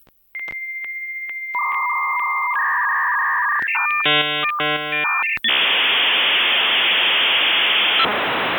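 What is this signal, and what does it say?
Dial-up modem handshake.
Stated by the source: Dial-up modem sound from the game Progressbar95 soundtrack (gamerip).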